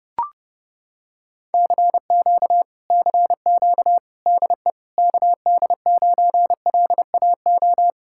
Morse code, heard as a keyed tone, about 30 wpm, 700 Hz. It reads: CQ CQ DE KD9LAO